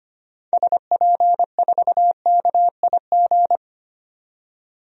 Morse code, heard as a keyed tone, about 25 wpm, 700 Hz. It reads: SP4KIG
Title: DXCC-100 Call Signs - 6 Characters - 25wpm